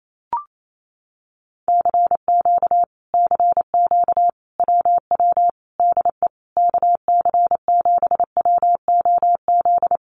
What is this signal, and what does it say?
Morse code, heard as a keyed tone, about 28 wpm, 700 Hz. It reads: CQ CQ WW DE KC7WOZ
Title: Common QSO Phrases Encore 28wpm